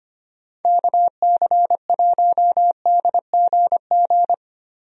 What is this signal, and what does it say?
Morse code, heard as a keyed tone, about 25 wpm, 700 Hz. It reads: KC1DGG
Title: USA Call Signs - 6 Characters - Repeated 2x - 25wpm